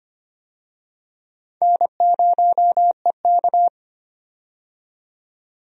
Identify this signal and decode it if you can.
Morse code, heard as a keyed tone, about 25 wpm, 700 Hz. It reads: N0EK
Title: USA Call Signs - 4 Characters - 25wpm